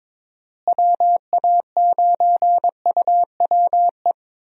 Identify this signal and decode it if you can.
Morse code, heard as a keyed tone, about 22 wpm, 700 Hz. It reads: WA9UWE
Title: USA Call Signs - 6 Characters - Repeated 3x - 22wpm